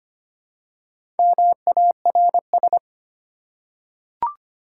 Morse code, heard as a keyed tone, about 25 wpm, 700 Hz. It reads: MARS
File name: practice / 25wpm / Easy Words with Hints - 25wpm